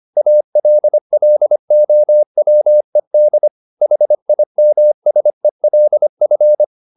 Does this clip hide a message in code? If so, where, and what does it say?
Yes: Morse code from 0.2 to 6.6 s, heard as a keyed tone, about 25 wpm, 600 Hz. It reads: ALLOWED HIMSELF